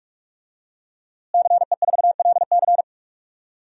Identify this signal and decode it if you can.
Morse code, heard as a keyed tone, about 45 wpm, 700 Hz. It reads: CE4LC
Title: DXCC-100 Call Signs - 5 Characters - Repeated 2x - 45wpm